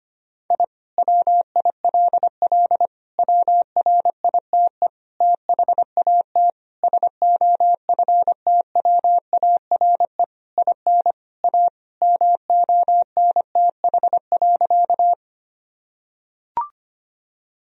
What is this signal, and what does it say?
Morse code, heard as a keyed tone, about 25 wpm, 700 Hz. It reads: I WILL WRITE THAT SOFTWARE IN A MONTH.
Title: Sentences from Top 1000 Words - Review Part 3 - 25wpm